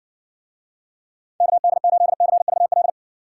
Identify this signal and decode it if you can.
Morse code, heard as a keyed tone, about 59 wpm, 700 Hz. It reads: KD8CVL